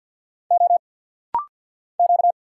Morse code, heard as a keyed tone, about 50 wpm, 700 Hz. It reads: O Q
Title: Single Letter-Number Alt - Rapid-Fire 50wpm